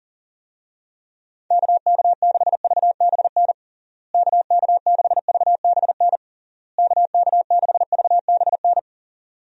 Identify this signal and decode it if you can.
Morse code, heard as a keyed tone, about 40 wpm, 700 Hz. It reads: KK6VBN KK6VBN KK6VBN
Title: USA Call Signs - 6 Characters - Repeated 3x - 40wpm